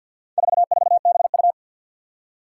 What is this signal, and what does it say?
Morse code, heard as a keyed tone, about 52 wpm, 700 Hz. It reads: 44BU
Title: Four Challenging Letter-Number Combinations 50wpm